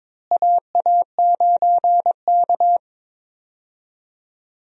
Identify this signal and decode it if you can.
Morse code, heard as a keyed tone, about 21 wpm, 700 Hz. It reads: AA9K
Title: USA Call Signs - 4 Characters - Repeated 2x - 22wpm